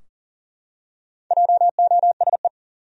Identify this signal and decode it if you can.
Morse code, heard as a keyed tone, about 40 wpm, 700 Hz. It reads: JOSE
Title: USA Common Names - Max 5 Letters - 40wpm